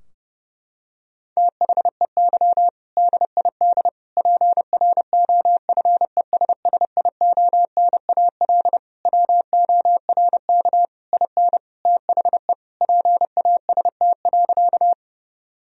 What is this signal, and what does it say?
Morse code, heard as a keyed tone, about 30 wpm, 700 Hz. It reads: THEY DID PROFESSIONAL WORK IN THE PAST.
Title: Sentences from Top 1000 Words - Review Part 4 - 30wpm